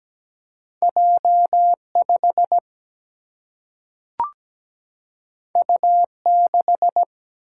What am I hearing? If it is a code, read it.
Morse code, heard as a keyed tone, about 17 wpm, 700 Hz. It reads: J5 U6
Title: Single-Letter Single-Number 17wpm